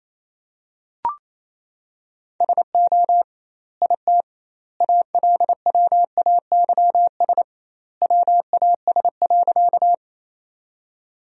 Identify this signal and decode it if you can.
Morse code, heard as a keyed tone, about 28 wpm, 700 Hz. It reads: SO IT ALWAYS WAS.
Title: 2x Word-Spacing - Sentences from Top 300 Words - 28wpm